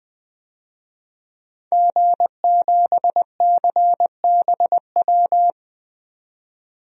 Morse code, heard as a keyed tone, about 20 wpm, 700 Hz. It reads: G7CBW
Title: DXCC-100 Call Signs - 5 Characters - 20wpm